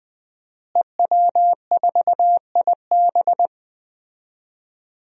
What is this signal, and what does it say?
Morse code, heard as a keyed tone, about 20 wpm, 700 Hz. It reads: EW4IB
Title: DXCC-100 Call Signs - 5 Characters - Repeated 3x - 20wpm